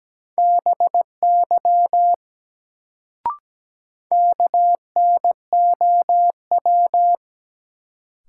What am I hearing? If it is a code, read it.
Morse code, heard as a keyed tone, about 17 wpm, 700 Hz. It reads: BY KNOW
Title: Sentences from Top 100 Words - New Word Drill - 17wpm